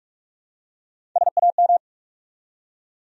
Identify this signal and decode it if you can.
Morse code, heard as a keyed tone, about 47 wpm, 700 Hz. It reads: SAM
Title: USA Common Names - Max 4 Letters - 45wpm